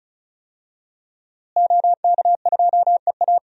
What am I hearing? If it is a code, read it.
Morse code, heard as a keyed tone, about 34 wpm, 700 Hz. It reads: OK2EA
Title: DXCC-100 Call Signs - 5 Characters - Repeated 2x - 35wpm